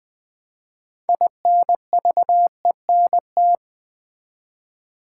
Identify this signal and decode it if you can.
Morse code, heard as a keyed tone, about 20 wpm, 700 Hz. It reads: INVENT
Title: 6 Letter Words - First 3 Letters Given - 20wpm